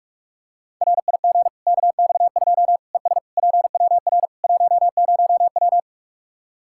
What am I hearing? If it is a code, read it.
Morse code, heard as a keyed tone, about 46 wpm, 700 Hz. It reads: RIG KX2 ES PWR 10W
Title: K1USN SST QSO Practice 45wpm